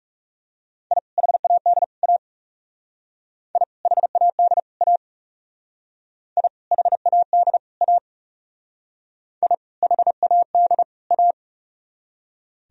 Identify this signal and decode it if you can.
Morse code, heard as a keyed tone, about 35 wpm, 700 Hz. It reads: I HAD A I HAD A I HAD A I HAD A
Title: Speed-Racing - Sets of 3 Words 30wpm